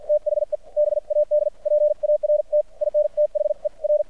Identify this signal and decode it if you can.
Morse code, heard as a keyed tone, about 53 wpm, spaced as for 46 wpm, 600 Hz. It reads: THE BAD PART IN THE R
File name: archived / CRSnet